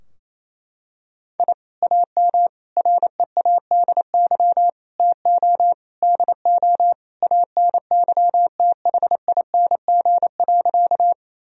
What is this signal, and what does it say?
Morse code, heard as a keyed tone, about 27 wpm, 700 Hz. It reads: I AM READY TO DO ANYTHING.
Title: Sentences from Top 1000 Words - Review Part 2 - 28wpm